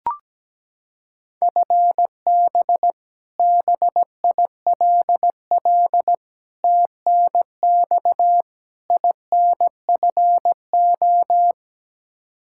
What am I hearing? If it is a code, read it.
Morse code, heard as a keyed tone, about 17 wpm, 700 Hz. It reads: FB BILL TNX INFO
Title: Practice QSO - Exchange 3 - More Info - 17wpm